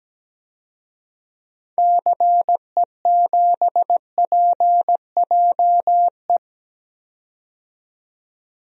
Morse code, heard as a keyed tone, about 17 wpm, 700 Hz. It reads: CE7PJE
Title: DXCC-100 Call Signs - 6 Characters - 17wpm